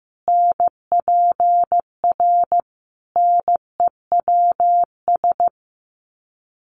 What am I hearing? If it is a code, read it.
Morse code, heard as a keyed tone, about 15 wpm, 700 Hz. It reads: NPR NEWS
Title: Sets of 5 Words Encore 15wpm